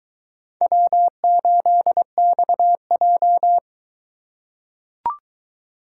Morse code, heard as a keyed tone, about 22 wpm, 700 Hz. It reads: W8XJ